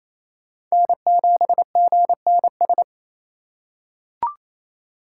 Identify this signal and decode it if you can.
Morse code, heard as a keyed tone, about 28 wpm, 700 Hz. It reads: N7GNS
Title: USA Call Signs - 5 Characters - 8x Character Spacing - 28wpm